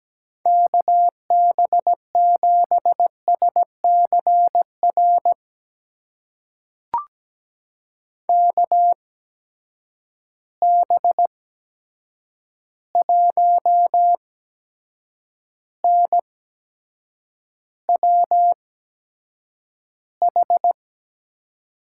Morse code, heard as a keyed tone, about 17 wpm, spaced as for 7 wpm, 700 Hz. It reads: KB7SCR KB1NWH
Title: USA Call Signs - 6 Characters - 8x Character Spacing - 17wpm